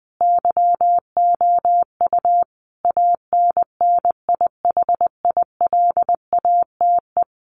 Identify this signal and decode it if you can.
Morse code, heard as a keyed tone, about 20 wpm, 700 Hz. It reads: YOU ANNIHILATE
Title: A Sound Of Thunder 20wpm